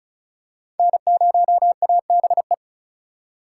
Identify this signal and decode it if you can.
Morse code, heard as a keyed tone, about 36 wpm, 700 Hz. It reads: N0ABE
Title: USA Call Signs - 5 Characters - 4x Character Spacing - 35wpm